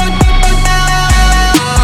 There is no code.